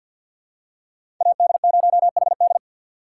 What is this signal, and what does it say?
Morse code, heard as a keyed tone, about 50 wpm, 700 Hz. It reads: AD0HD
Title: USA Call Signs - 5 Characters - 3x Character Spacing - 50wpm